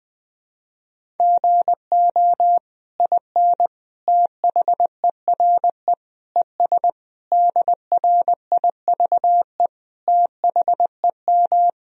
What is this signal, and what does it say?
Morse code, heard as a keyed tone, about 20 wpm, 700 Hz. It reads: GO IN THERE ES DRIVE THEM
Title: Sentences from Top 500 Words and Abbreviations 20wpm